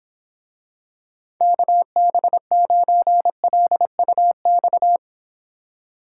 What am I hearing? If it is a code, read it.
Morse code, heard as a keyed tone, about 26 wpm, 700 Hz. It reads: KB9LUX